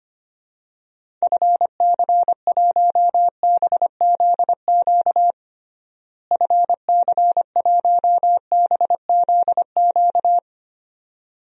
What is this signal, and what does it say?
Morse code, heard as a keyed tone, about 25 wpm, 700 Hz. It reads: FC1BZQ FC1BZQ
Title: DXCC-100 Call Signs - 6 Characters - Repeated 2x - 25wpm